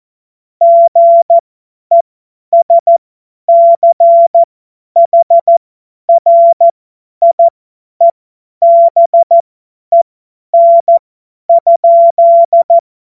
Morse code, heard as a keyed tone, about 14 wpm, spaced as for 10 wpm, 680 Hz. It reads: GESCHRIEBEN?
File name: Morsekurs / Witze / Witz 13. September 2022 50 F70